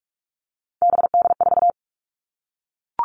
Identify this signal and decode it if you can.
Morse code, heard as a keyed tone, about 44 wpm, 700 Hz. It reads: BD4
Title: All Cut Numbers - 3-Digit 45wpm